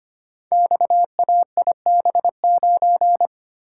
Morse code, heard as a keyed tone, about 25 wpm, 700 Hz. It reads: XAIB9
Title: Code Groups - 5 Letter:Numbers - 25wpm